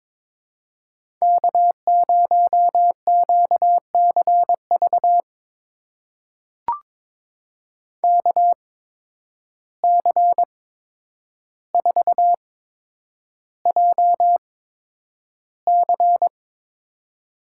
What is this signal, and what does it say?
Morse code, heard as a keyed tone, about 22 wpm, spaced as for 9 wpm, 700 Hz. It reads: K0QCV KC4JC
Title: USA Call Signs - 5 Characters - 8x Character Spacing - 22wpm